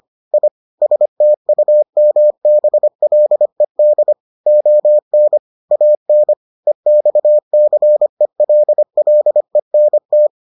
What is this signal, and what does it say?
Morse code, heard as a keyed tone, about 25 wpm, 600 Hz. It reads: I STUMBLED ON AN EXCELLENT